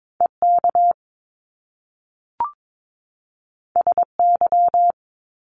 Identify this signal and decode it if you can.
Morse code, heard as a keyed tone, about 22 wpm, 700 Hz. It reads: EK SY
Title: Top Two Letter Combinations in Words 22wpm